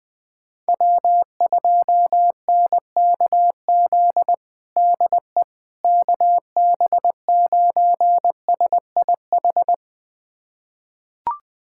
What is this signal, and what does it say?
Morse code, heard as a keyed tone, about 20 wpm, 700 Hz. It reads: W2NKZ DE KB9SIH
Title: USA Callsign DE Callsign 20wpm